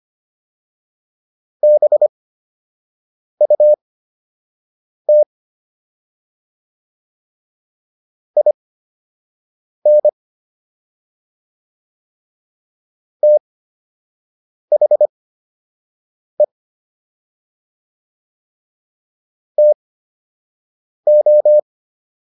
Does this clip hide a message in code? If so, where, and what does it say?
Yes: Morse code from 1.6 to 21.6 s, heard as a keyed tone, about 25 wpm, spaced as for 6 wpm, 600 Hz. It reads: BUT IN THE TO